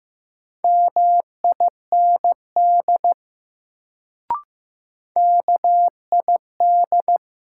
Morse code, heard as a keyed tone, about 15 wpm, 700 Hz. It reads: MIND KID
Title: Sets of 5 Words - New Word Drill - 15wpm